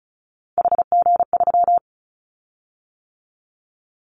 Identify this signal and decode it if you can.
Morse code, heard as a keyed tone, about 36 wpm, 700 Hz. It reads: HG3
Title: Callsign Prefixes for Top 50 DXCC - 35wpm